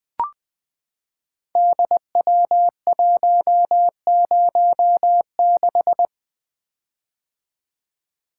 Morse code, heard as a keyed tone, about 20 wpm, 700 Hz. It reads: DW106
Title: Code Groups - 5 Letter:Numbers - 20wpm